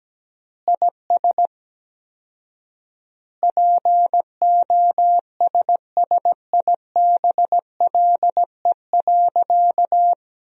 Morse code, heard as a keyed tone, about 17 wpm, 700 Hz. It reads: IS POSSIBLE.
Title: 4x Word-Spacing - Sentences from Top 500 Words - 17wpm